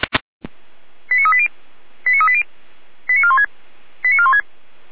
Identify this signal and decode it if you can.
Five-tone selective call (ZVEI), heard as quick runs of five tones, tones of about 80 ms. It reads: 89380 89380 89416 89416